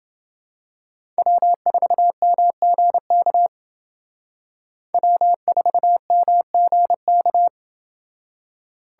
Morse code, heard as a keyed tone, about 28 wpm, 700 Hz. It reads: W4MGK W4MGK